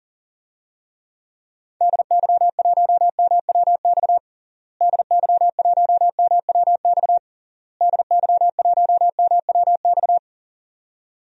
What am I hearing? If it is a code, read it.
Morse code, heard as a keyed tone, about 37 wpm, 700 Hz. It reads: DY1MWX DY1MWX DY1MWX
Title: DXCC-100 Call Signs - 6 Characters - Repeated 3x - 40wpm